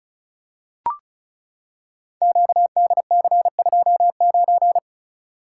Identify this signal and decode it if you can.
Morse code, heard as a keyed tone, about 36 wpm, 700 Hz. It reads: QDC29
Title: Code Groups - 5 Letter:Numbers - 35wpm